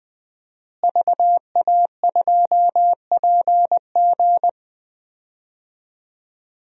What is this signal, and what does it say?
Morse code, heard as a keyed tone, about 21 wpm, 700 Hz. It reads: VA2PG
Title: DXCC-100 Call Signs - 5 Characters - Repeated 3x - 20wpm